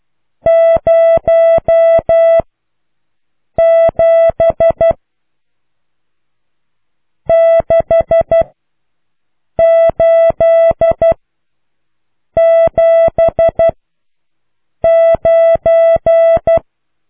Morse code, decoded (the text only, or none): 0 7 6 8 7 9